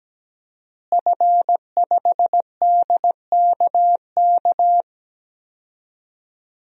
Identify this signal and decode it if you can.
Morse code, heard as a keyed tone, about 17 wpm, 700 Hz. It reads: F5DKK
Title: DXCC-100 Call Signs - 5 Characters - Repeated 3x - 17wpm